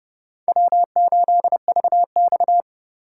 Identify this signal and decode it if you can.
Morse code, heard as a keyed tone, about 29 wpm, 700 Hz. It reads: W8VX